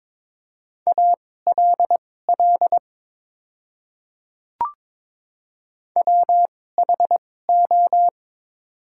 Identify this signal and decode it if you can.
Morse code, heard as a keyed tone, about 22 wpm, spaced as for 16 wpm, 700 Hz. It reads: ALL WHO
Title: 2x Character Spacing - Top 100 Words - 22wpm